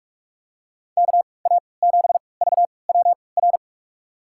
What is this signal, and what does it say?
Morse code, heard as a keyed tone, about 46 wpm, spaced as for 26 wpm, 700 Hz. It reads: KA7VWR
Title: USA Call Signs - 3x Character Spacing - 45wpm